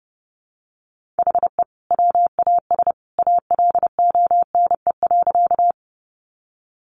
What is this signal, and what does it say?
Morse code, heard as a keyed tone, about 30 wpm, 700 Hz. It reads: HE WAS ALONE.
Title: Sentences from Top 1000 Words 30wpm